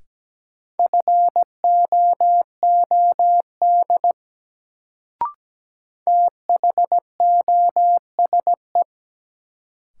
Morse code, heard as a keyed tone, about 17 wpm, 700 Hz. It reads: FOOD THOSE
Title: Sentences from Top 300 Words - New Word Drill - 17wpm